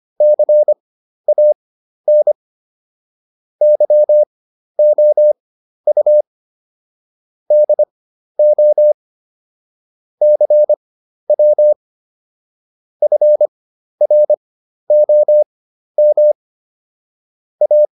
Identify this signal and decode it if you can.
Morse code, heard as a keyed tone, about 25 wpm, spaced as for 12 wpm, 600 Hz. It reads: CAN YOU DO CW FROM A